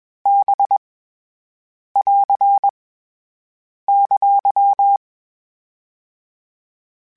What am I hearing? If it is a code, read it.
Morse code, heard as a keyed tone, about 21 wpm, 800 Hz. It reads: B <AR> !
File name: Basic Sets / Basic Letter Groups Collective7 / VB46@20